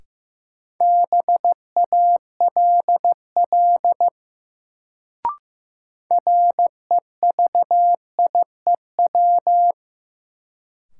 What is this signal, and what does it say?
Morse code, heard as a keyed tone, about 15 wpm, 700 Hz. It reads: BALL REVIEW
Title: Sentences from Top 900 Words - New Word Drill - 15wpm